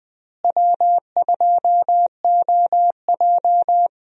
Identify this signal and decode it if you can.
Morse code, heard as a keyed tone, about 21 wpm, 700 Hz. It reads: W2OJ
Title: USA Call Signs - 4 Characters - Repeated 3x - 20wpm